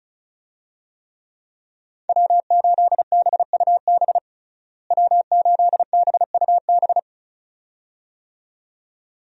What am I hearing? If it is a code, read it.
Morse code, heard as a keyed tone, about 36 wpm, 700 Hz. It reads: W8BUB W8BUB
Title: USA Call Signs - 5 Characters - Repeated 2x - 35wpm